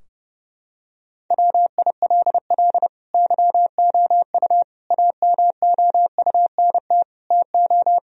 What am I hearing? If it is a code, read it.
Morse code, heard as a keyed tone, about 30 wpm, 700 Hz. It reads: WILL YOU AMOUNT TO
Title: Sentences from Top 1000 Words - Review Part 4 - 30wpm